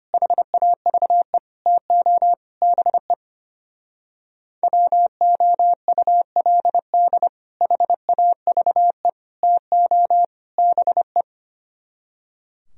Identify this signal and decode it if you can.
Morse code, heard as a keyed tone, about 28 wpm, 700 Hz. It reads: HAVE TO BE WOULD HAVE TO BE
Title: Speed-Racing - Sets of 4 Words 25wpm